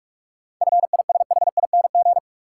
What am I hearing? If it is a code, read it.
Morse code, heard as a keyed tone, about 44 wpm, 700 Hz. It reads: FISHING
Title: Sentences from Top 1600 Words - New Word Drill Alt - 45wpm